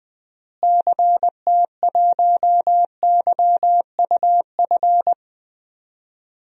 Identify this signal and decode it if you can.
Morse code, heard as a keyed tone, about 20 wpm, 700 Hz. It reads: CT1YUF
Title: DXCC-100 Call Signs - 6 Characters - Repeated 3x - 20wpm